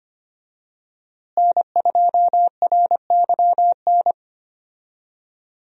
Morse code, heard as a keyed tone, about 25 wpm, 700 Hz. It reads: N2RYN
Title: USA Call Signs - 5 Characters - Repeated 3x - 25wpm